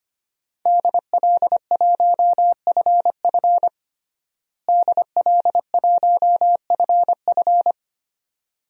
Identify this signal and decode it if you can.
Morse code, heard as a keyed tone, about 25 wpm, 700 Hz. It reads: DL1FF DL1FF
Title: DXCC-100 Call Signs - 5 Characters - Repeated 2x - 25wpm